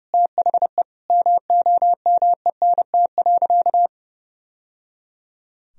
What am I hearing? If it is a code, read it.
Morse code, heard as a keyed tone, about 30 wpm, 700 Hz. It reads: THE MOMENT.
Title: Sentences from Top 1000 Words - Review Part 3 - 30wpm